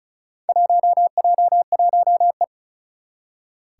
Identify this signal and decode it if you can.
Morse code, heard as a keyed tone, about 36 wpm, 700 Hz. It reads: 1J1E
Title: Four Challenging Letter-Number Combinations 35wpm